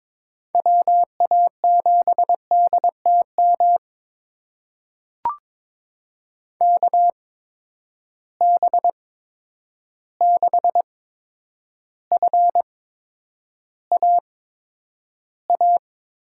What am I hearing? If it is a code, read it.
Morse code, heard as a keyed tone, about 22 wpm, spaced as for 9 wpm, 700 Hz. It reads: WA7DTM KB6FAA
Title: USA Call Signs - 6 Characters - 8x Character Spacing - 22wpm